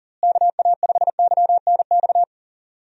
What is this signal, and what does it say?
Morse code, heard as a keyed tone, about 40 wpm, 700 Hz. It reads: KA5YNX
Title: USA Call Signs - 6 Characters - Repeated 2x - 40wpm